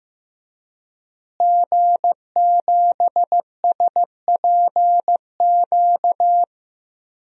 Morse code, heard as a keyed tone, about 15 wpm, 700 Hz. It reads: G7SPQ
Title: DXCC-100 Call Signs - 5 Characters - Repeated 2x - 15wpm